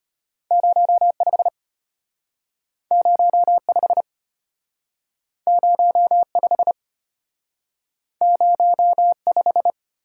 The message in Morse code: 05 05 05 05

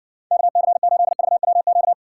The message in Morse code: KC8VWQ

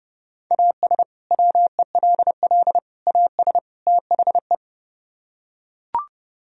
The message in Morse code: AS WELL AS THE

